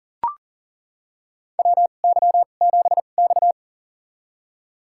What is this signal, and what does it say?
Morse code, heard as a keyed tone, about 40 wpm, spaced as for 29 wpm, 700 Hz. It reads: WY7X
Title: USA Call Signs - 4 Characters - 2x Character Spacing - 40wpm